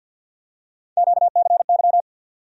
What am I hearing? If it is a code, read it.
Morse code, heard as a keyed tone, about 47 wpm, 690 Hz. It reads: QCY
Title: USA Call Sign Postfixes - 3 Letters 50wpm